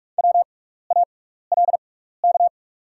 Morse code, heard as a keyed tone, about 46 wpm, spaced as for 16 wpm, 690 Hz. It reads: WALK
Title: 6x Character Spacing - Top 300 Words - 45wpm